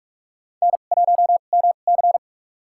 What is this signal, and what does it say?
Morse code, heard as a keyed tone, about 46 wpm, spaced as for 32 wpm, 690 Hz. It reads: N1MC
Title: USA Call Signs - 4 Characters - 2x Character Spacing - 45wpm